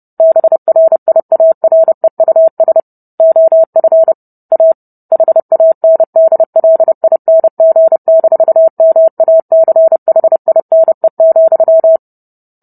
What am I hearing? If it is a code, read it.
Morse code, heard as a keyed tone, about 30 wpm, 650 Hz. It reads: BRIAREUS OF A HANDLING-MACHINE,